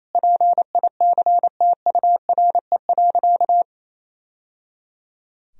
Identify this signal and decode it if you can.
Morse code, heard as a keyed tone, about 28 wpm, 700 Hz. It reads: PICTURE.
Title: Sentences from Top 1000 Words - Review Part 2 - 28wpm